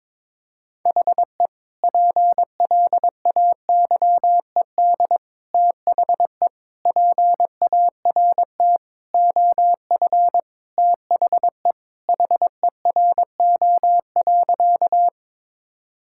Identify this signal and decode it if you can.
Morse code, heard as a keyed tone, about 22 wpm, 700 Hz. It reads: HE PLAYED THE PART OF THE HERO.
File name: practice / 22wpm / Sentences with 4-Word Phrases - 22wpm